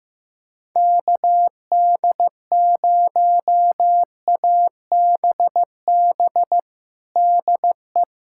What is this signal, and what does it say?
Morse code, heard as a keyed tone, about 16 wpm, 700 Hz. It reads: KD0ABB DE
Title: USA Callsign DE Callsign - 6 Characters - 15wpm